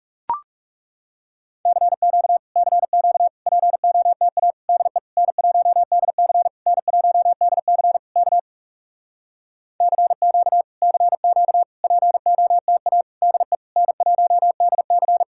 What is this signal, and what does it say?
Morse code, heard as a keyed tone, about 40 wpm, 700 Hz. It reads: CQ CQ POTA DE N1DC N1DC K CQ CQ POTA DE N1DC